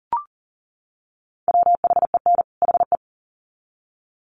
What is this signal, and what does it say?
Morse code, heard as a keyed tone, about 40 wpm, 700 Hz. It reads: WHEN HE
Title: Sets of 2 Words 40wpm